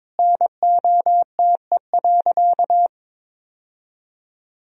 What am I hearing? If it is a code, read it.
Morse code, heard as a keyed tone, about 22 wpm, 700 Hz. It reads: NOTE.
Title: Sentences from Top 1000 Words - Review Part 2 - 22wpm